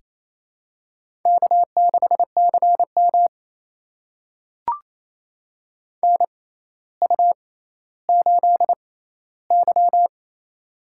Morse code, heard as a keyed tone, about 27 wpm, spaced as for 14 wpm, 700 Hz. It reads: K6CM NU8Y
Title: USA Call Signs - 4 Characters - 6x Character Spacing - 28wpm